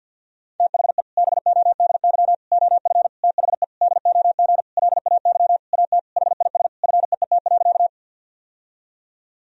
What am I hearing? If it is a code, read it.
Morse code, heard as a keyed tone, about 52 wpm, 700 Hz. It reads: THE BODY OF THE DOG LAY AT HIS FEET.
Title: Sentences from Top 1000 Words - Review Part 2 - 50wpm